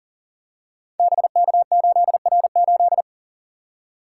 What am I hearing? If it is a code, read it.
Morse code, heard as a keyed tone, about 42 wpm, 700 Hz. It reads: BK8R8
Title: Code Groups - 5 Letter:Numbers - 40wpm